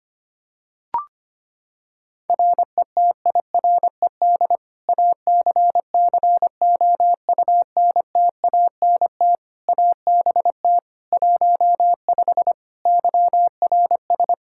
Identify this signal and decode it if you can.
Morse code, heard as a keyed tone, about 25 wpm, 700 Hz. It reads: RETIRED ACCOUNTANT ABT 15 YRS
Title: Practice QSO - Exchange 3 - More Info - 25wpm